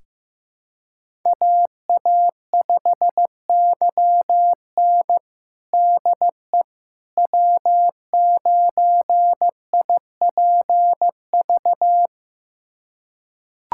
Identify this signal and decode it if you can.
Morse code, heard as a keyed tone, about 15 wpm, 700 Hz. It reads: AA5YN DE W9IPV
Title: USA Callsign DE Callsign - 5 Characters - 15wpm